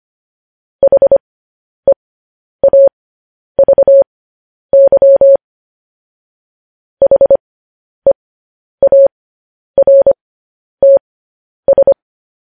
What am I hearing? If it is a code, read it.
Morse code, heard as a keyed tone, about 25 wpm, spaced as for 10 wpm, 550 Hz. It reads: HEAVY HEARTS
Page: News Headlines in Morse Code at 10 WPM